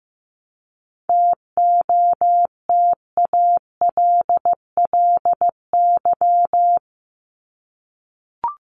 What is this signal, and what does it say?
Morse code, heard as a keyed tone, about 15 wpm, 700 Hz. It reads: TOTALLY
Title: Top 1501-1750 Words 15wpm